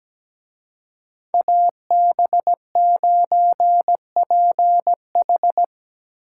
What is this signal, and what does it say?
Morse code, heard as a keyed tone, about 20 wpm, 700 Hz. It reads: AB9PH
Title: Speed Racing - USA 2x2 Call Signs 17wpm